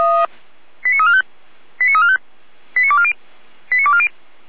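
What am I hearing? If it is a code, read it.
Five-tone selective call (ZVEI), heard as quick runs of five tones, tones of about 70 ms. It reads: 89345 89345 89240 89240
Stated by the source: Five-tone ZVEI selective call received on an SDR.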